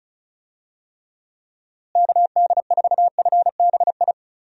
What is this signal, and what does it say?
Morse code, heard as a keyed tone, about 34 wpm, 700 Hz. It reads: KD4FBI
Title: USA Call Signs - 6 Characters - Repeated 3x - 35wpm